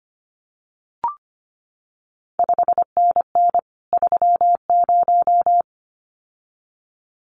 Morse code, heard as a keyed tone, about 25 wpm, 700 Hz. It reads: 5NN 30
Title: CQ WW DX Contest Exchange Encore - 25wpm